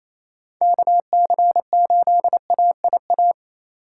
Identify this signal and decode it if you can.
Morse code, heard as a keyed tone, about 28 wpm, 700 Hz. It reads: KC8AIA
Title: USA Call Signs - 4x Character Spacing - 28wpm